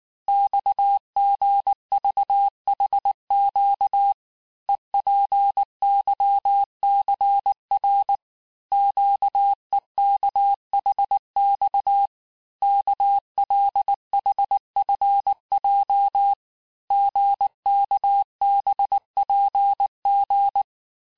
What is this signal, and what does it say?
Morse code, heard as a keyed tone, about 19 wpm, 800 Hz. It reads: XGVHQ EPYCR QEKHX KLHFJ GKBPG
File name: cours-du-mercredi / audios / cours-869-texte-CHIFFRE-vitesse-18-MPM